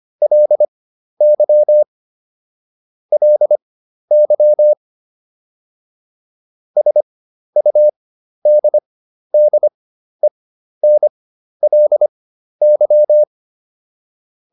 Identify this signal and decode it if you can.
Morse code, heard as a keyed tone, about 25 wpm, spaced as for 12 wpm, 600 Hz. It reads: LY LY SUDDENLY